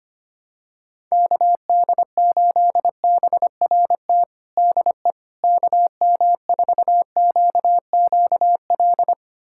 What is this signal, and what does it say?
Morse code, heard as a keyed tone, about 25 wpm, 700 Hz. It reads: KD8BRT DE KM4QQL
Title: USA Callsign DE Callsign 25wpm